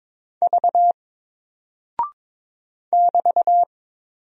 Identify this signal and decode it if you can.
Morse code, heard as a keyed tone, about 22 wpm, 700 Hz. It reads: V <BT>
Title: Single Character (Letter:Number:Punctuation:Prosign) 22wpm